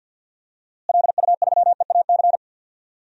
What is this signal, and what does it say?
Morse code, heard as a keyed tone, about 52 wpm, 700 Hz. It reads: LU3EAC
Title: DXCC-100 Call Signs - 6 Characters - Repeated 2x - 50wpm